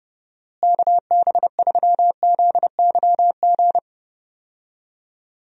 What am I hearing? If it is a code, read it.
Morse code, heard as a keyed tone, about 30 wpm, 700 Hz. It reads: KB3ZYG
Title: USA Call Signs - 6 Characters - Repeated 3x - 30wpm